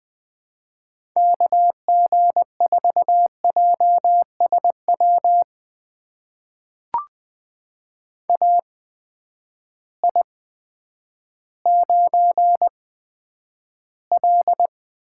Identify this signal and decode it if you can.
Morse code, heard as a keyed tone, about 20 wpm, spaced as for 10 wpm, 700 Hz. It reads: KG4JSW AI9L
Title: USA Call Signs - 8x Character Spacing - 20wpm